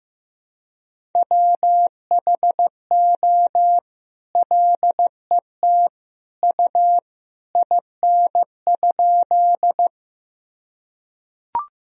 Morse code, heard as a keyed tone, about 16 wpm, 700 Hz. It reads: WHO LET U IN?
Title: Sentences from Top 300 Words and Abbreviations 15wpm